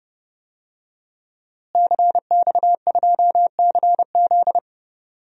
Morse code, heard as a keyed tone, about 30 wpm, 700 Hz. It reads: CX2CZ
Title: DXCC-100 Call Signs - 5 Characters - Repeated 3x - 30wpm